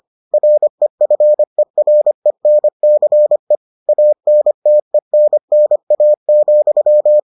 Morse code, heard as a keyed tone, about 25 wpm, 600 Hz. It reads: REFERENCE ANTENNA,